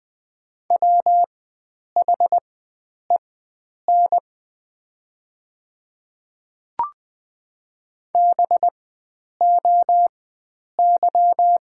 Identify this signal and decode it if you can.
Morse code, heard as a keyed tone, about 20 wpm, spaced as for 9 wpm, 700 Hz. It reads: WHEN BOY
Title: 4x Character Spacing - Top 300 Words - 20wpm